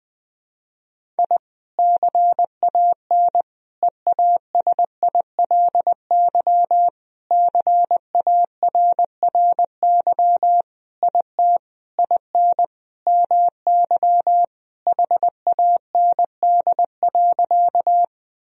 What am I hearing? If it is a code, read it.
Morse code, heard as a keyed tone, about 21 wpm, 700 Hz. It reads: I CAN EASILY CARRY IT IN MY HAND.
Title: Sentences from Top 1000 Words - Review Part 1 - 20wpm